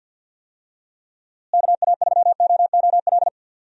Morse code, heard as a keyed tone, about 46 wpm, 700 Hz. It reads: KA3OOL
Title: USA Call Signs - 6 Characters - Repeated 3x - 50wpm